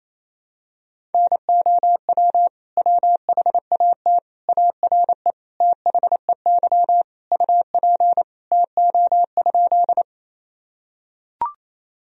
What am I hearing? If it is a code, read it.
Morse code, heard as a keyed tone, about 28 wpm, 700 Hz. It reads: NOW WHAT ARE THEY UP TO?
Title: Sentences from Top 1000 Words - Review Part 1 - 28wpm